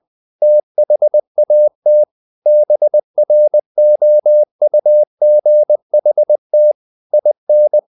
Morse code, heard as a keyed tone, about 20 wpm, 600 Hz. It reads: THAT BROUGHT IN
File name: the old man and the sea part 2 20wpm